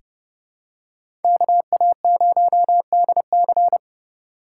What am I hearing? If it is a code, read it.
Morse code, heard as a keyed tone, about 29 wpm, 700 Hz. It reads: KA0DC
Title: USA Call Signs - 5 Characters - 4x Character Spacing - 30wpm